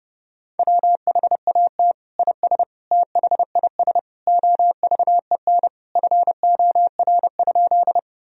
Morse code, heard as a keyed tone, about 30 wpm, 700 Hz. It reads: WHAT IS THIS OVEN FOR?
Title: Sentences from Top 1000 Words - Review Part 4 - 30wpm